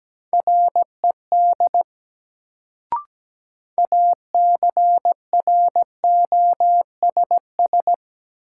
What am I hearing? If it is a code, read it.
Morse code, heard as a keyed tone, about 17 wpm, 700 Hz. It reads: RED ACROSS
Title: Sentences from Top 700 Words - New Word Drill - 17wpm